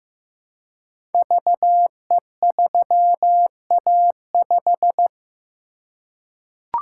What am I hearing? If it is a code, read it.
Morse code, heard as a keyed tone, about 15 wpm, 700 Hz. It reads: VE3A5
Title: Code Groups - 5 Letter:Numbers - 15wpm